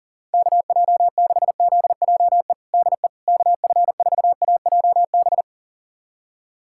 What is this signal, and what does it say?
Morse code, heard as a keyed tone, about 37 wpm, 700 Hz. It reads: KJ6ZJE DE KF4AJB
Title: USA Callsign DE Callsign 40wpm